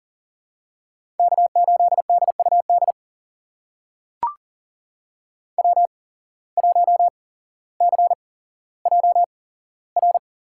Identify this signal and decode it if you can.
Morse code, heard as a keyed tone, about 40 wpm, spaced as for 17 wpm, 700 Hz. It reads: K8DUD W1CJR
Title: USA Call Signs - 5 Characters - 8x Character Spacing - 40wpm